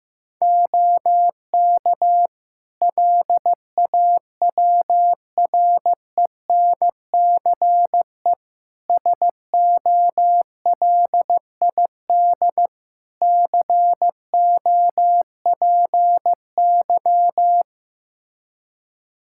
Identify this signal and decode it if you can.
Morse code, heard as a keyed tone, about 15 wpm, 700 Hz. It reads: OK LAWRENCE SOLID COPY
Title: Practice QSO - Exchange 3 - More Info - 15wpm